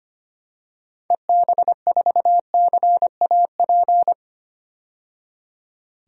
Morse code, heard as a keyed tone, about 25 wpm, 700 Hz. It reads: EB4CAP
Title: DXCC-100 Call Signs - 6 Characters - 25wpm